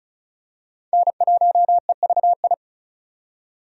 Morse code, heard as a keyed tone, about 36 wpm, 700 Hz. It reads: N1EVI
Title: USA Call Signs - 5 Characters - 35wpm